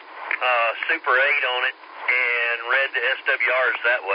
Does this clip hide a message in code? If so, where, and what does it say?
Yes: Emergency Alert System (SAME) tones from 0.8 to 3.5 s, heard as harsh buzzing bursts, with speech.